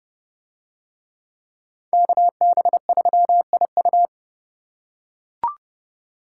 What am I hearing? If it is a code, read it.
Morse code, heard as a keyed tone, about 29 wpm, 700 Hz. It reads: KB3IU